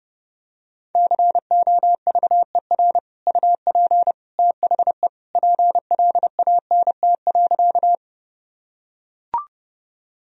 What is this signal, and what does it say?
Morse code, heard as a keyed tone, about 32 wpm, 700 Hz. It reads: COVER UP THE PLANT.
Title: Sentences from Top 1000 Words - Review Part 1 - 30wpm